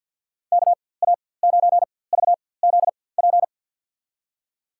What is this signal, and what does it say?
Morse code, heard as a keyed tone, about 46 wpm, spaced as for 23 wpm, 700 Hz. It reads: KA9VZP
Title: USA Call Signs - 4x Character Spacing - 50wpm